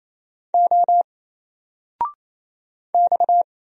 Morse code, heard as a keyed tone, about 29 wpm, 700 Hz. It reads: O X
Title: Single Letter-Number 28wpm